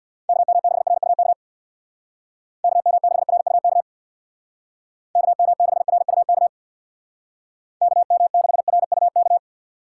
Morse code, heard as a keyed tone, about 62 wpm, 700 Hz. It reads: KM6RUK KM6RUK KM6RUK KM6RUK
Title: Speed Racing - USA Call Signs 50wpm